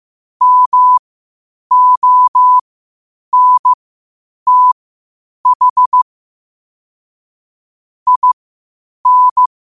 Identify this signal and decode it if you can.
Morse code, heard as a keyed tone, about 15 wpm, spaced as for 8 wpm, 1000 Hz. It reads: MONTH IN
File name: qotdfiles / 8wpm